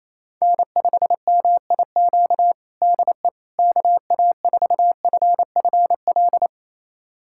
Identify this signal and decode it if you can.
Morse code, heard as a keyed tone, about 28 wpm, 700 Hz. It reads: N5MIQ DE KA4FFL
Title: USA Callsign DE Callsign 28wpm